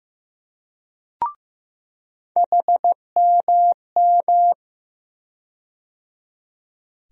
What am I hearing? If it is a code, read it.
Morse code, heard as a keyed tone, about 15 wpm, 700 Hz. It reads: HMM
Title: USA Call Sign Postfixes - 3 Letters 15wpm